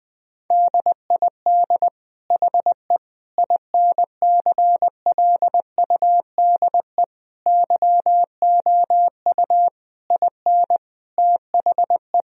Morse code, heard as a keyed tone, about 20 wpm, 700 Hz. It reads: DID HE INCLUDE YOU IN THE
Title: Sentences from Top 1000 Words - Review Part 4 - 20wpm